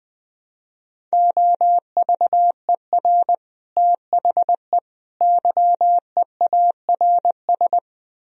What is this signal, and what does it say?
Morse code, heard as a keyed tone, about 20 wpm, 700 Hz. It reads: OVER THE YEARS
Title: Speed-Racing - Sets of 3 Words 17wpm